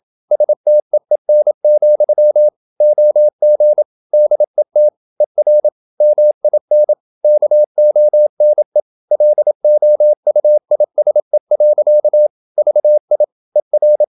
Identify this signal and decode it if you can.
Morse code, heard as a keyed tone, about 27 wpm, 600 Hz. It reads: STEEN, OG DET ER MIN KONE LOUISE. VI ER
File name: Ferie QRQ 27wpm